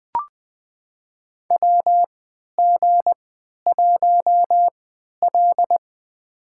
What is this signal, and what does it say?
Morse code, heard as a keyed tone, about 21 wpm, spaced as for 11 wpm, 700 Hz. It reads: WG1L
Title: USA Call Signs - 4 Characters - 3x Character Spacing - 20wpm